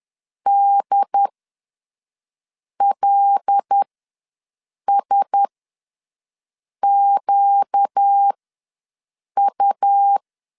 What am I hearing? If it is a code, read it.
Morse code, heard as a keyed tone, about 10 wpm, spaced as for 5 wpm, 800 Hz. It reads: DLSQU
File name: LEZIONI / 8 YJ45 / 8 YJ ws 10wpm 4043